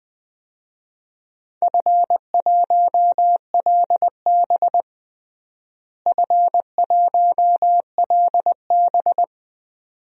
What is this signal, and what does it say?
Morse code, heard as a keyed tone, about 20 wpm, 700 Hz. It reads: F1LB F1LB
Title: DXCC-100 Call Signs - 4 Characters - Repeated 2x - 20wpm